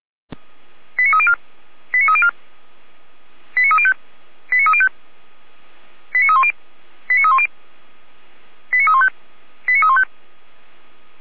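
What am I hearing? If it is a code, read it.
Five-tone selective call (ZVEI), heard as quick runs of five tones, tones of about 70 ms. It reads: 89304 89304 89305 89305 89310 89310 89315 89315
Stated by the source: ZVEI five-tone selective call code 89276, received by SDR, from the Feuerwehr Orlamünde fire brigade audio server.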